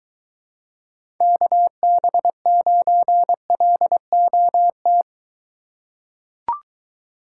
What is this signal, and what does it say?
Morse code, heard as a keyed tone, about 22 wpm, 700 Hz. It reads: KB9LOT